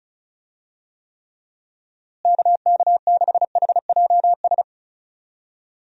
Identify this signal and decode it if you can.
Morse code, heard as a keyed tone, about 34 wpm, 700 Hz. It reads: KK6HJS
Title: USA Call Signs - 6 Characters - Repeated 2x - 35wpm